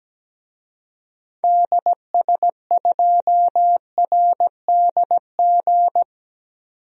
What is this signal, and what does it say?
Morse code, heard as a keyed tone, about 17 wpm, 700 Hz. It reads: DS2RDG
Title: DXCC-100 Call Signs - 6 Characters - 17wpm